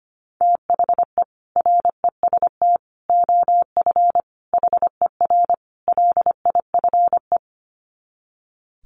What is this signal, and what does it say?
Morse code, heard as a keyed tone, about 25 wpm, 700 Hz. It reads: THE REST OF HER LIFE
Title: Sets of 5 Words Encore 25wpm